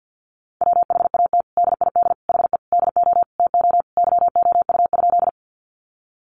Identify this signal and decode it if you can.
Morse code, heard as a keyed tone, about 50 wpm, 700 Hz. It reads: WHAT DID HE DO TO YOU?
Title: Sentences from Top 100 Words 50wpm